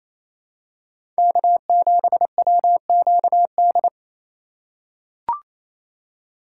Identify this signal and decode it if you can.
Morse code, heard as a keyed tone, about 28 wpm, 700 Hz. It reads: K7WQD
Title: USA Call Signs - 6x Character Spacing - 28wpm